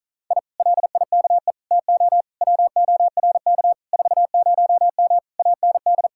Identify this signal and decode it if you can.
Morse code, heard as a keyed tone, about 46 wpm, 700 Hz. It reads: I LIKE TO WORK 40M AND